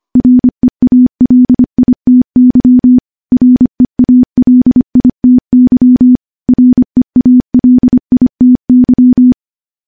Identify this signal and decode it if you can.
Morse code, heard as a keyed tone, about 25 wpm, 250 Hz. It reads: REALITY REALITY REALITY
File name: morse / morse-training-25WPM